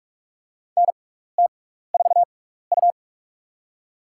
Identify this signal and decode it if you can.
Morse code, heard as a keyed tone, about 47 wpm, spaced as for 16 wpm, 700 Hz. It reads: NT4U
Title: USA Call Signs - 4 Characters - 6x Character Spacing - 45wpm